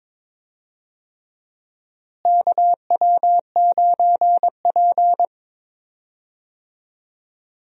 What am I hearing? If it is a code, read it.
Morse code, heard as a keyed tone, about 22 wpm, 700 Hz. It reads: KW9P
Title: USA Call Signs - 4 Characters - Repeated 3x - 22wpm